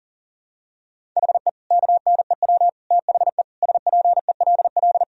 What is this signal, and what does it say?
Morse code, heard as a keyed tone, about 40 wpm, 700 Hz. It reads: HE KNEW THE SPELL
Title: Sentences from Top 1000 Words - Review Part 1 - 40wpm